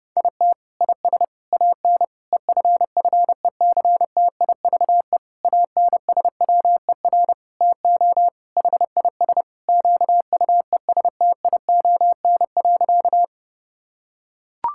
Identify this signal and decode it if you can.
Morse code, heard as a keyed tone, about 32 wpm, 700 Hz. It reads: IT IS AN EFFECTIVE ANSWER TO HIS QUESTION.
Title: Sentences from Top 1000 Words - Review Part 6 - 30wpm